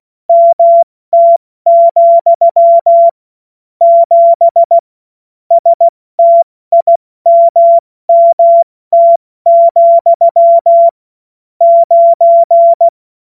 Morse code, heard as a keyed tone, about 16 wpm, 680 Hz. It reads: MT, 7 STIMMT, 9
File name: Morsekurs / Witze / Witz 12. April 2022 70 F80